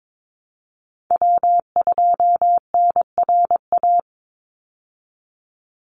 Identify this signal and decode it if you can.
Morse code, heard as a keyed tone, about 22 wpm, 700 Hz. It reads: W2NRA
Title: Call Signs Repeated 3x - 22wpm